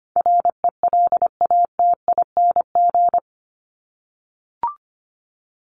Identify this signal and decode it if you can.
Morse code, heard as a keyed tone, about 25 wpm, 700 Hz. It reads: RELATING
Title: ING Suffix - Top 100 Words with Base Words 25wpm